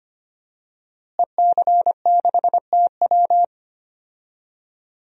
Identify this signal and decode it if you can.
Morse code, heard as a keyed tone, about 25 wpm, 700 Hz. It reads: EC6TW
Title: DXCC-100 Call Signs - 5 Characters - Repeated 2x - 25wpm